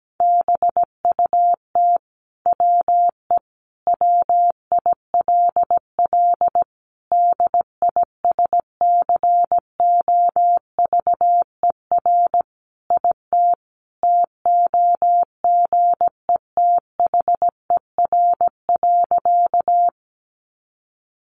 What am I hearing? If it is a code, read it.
Morse code, heard as a keyed tone, about 17 wpm, 700 Hz. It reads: BUT WE WILL DISCOVER IT TOGETHER.
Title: Sentences from Top 1100 Words 17wpm